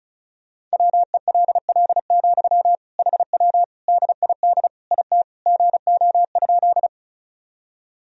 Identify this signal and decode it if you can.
Morse code, heard as a keyed tone, about 34 wpm, 700 Hz. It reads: WELL, HW DID IT GO?
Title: Sentences from Top 200 Words and Abbreviations 35wpm